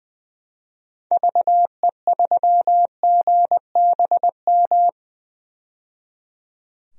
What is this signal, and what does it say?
Morse code, heard as a keyed tone, about 20 wpm, 700 Hz. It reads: VE3GBM
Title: DXCC-100 Call Signs - 6 Characters - 20wpm